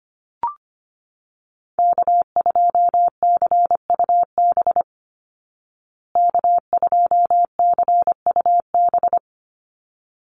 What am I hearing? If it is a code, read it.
Morse code, heard as a keyed tone, about 25 wpm, 700 Hz. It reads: K2CUB K2CUB
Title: Call Signs Repeated 3x - 25wpm